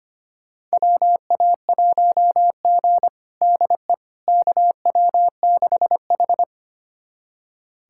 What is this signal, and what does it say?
Morse code, heard as a keyed tone, about 25 wpm, 700 Hz. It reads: WA1G DE KW6H
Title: USA Callsign DE Callsign - 4 Characters - 25wpm